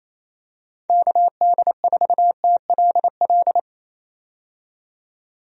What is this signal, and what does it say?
Morse code, heard as a keyed tone, about 28 wpm, 700 Hz. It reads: KD4TLL
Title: USA Call Signs - 6 Characters - 28wpm